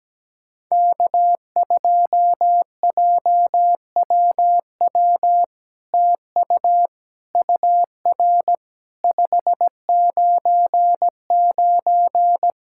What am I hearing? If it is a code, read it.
Morse code, heard as a keyed tone, about 17 wpm, 700 Hz. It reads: K2JWW TU UR 599
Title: SOTA QSO Practice 17wpm